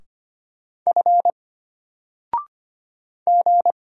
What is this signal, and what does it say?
Morse code, heard as a keyed tone, about 25 wpm, 700 Hz. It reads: F G
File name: Single-Letter-Number-25wpm